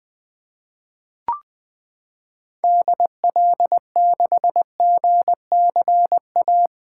Morse code, heard as a keyed tone, about 20 wpm, 700 Hz. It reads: DL6GCA
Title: 2023 SOTA Activator Call Signs - 20wpm